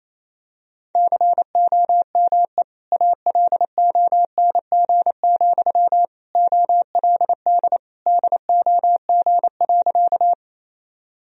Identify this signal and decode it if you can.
Morse code, heard as a keyed tone, about 28 wpm, 700 Hz. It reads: COME ALONG, OLD DOG.
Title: Sentences from Top 1000 Words - Review Part 2 - 28wpm